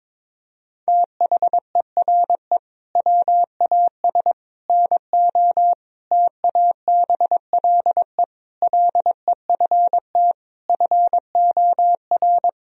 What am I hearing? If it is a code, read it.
Morse code, heard as a keyed tone, about 22 wpm, 700 Hz. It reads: THERE WAS NO TABLE LEFT FOR
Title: Sentences from Top 1000 Words - Review Part 2 - 22wpm